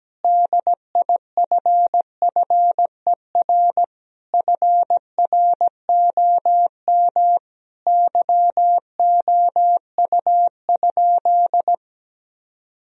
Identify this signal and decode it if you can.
Morse code, heard as a keyed tone, about 17 wpm, 700 Hz. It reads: DIFFER FROM YOU?
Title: Sentences from Top 1000 Words - Review Part 1 - 17wpm